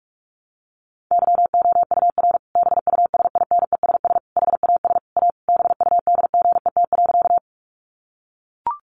Morse code, heard as a keyed tone, about 44 wpm, 700 Hz. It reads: YOUR BUSINESS HAS A BUDGET.
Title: Sentences from Top 1200 Words 45wpm